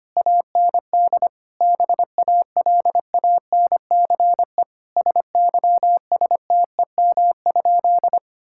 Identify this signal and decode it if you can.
Morse code, heard as a keyed tone, about 25 wpm, 700 Hz. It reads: AND BALANCE SYSTEM?
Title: Sentences from Top 1000 Words - Review Part 6 - 25wpm